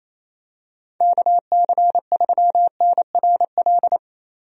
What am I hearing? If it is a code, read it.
Morse code, heard as a keyed tone, about 28 wpm, 700 Hz. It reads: KC3NRL
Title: USA Call Signs - 6 Characters - Repeated 3x - 28wpm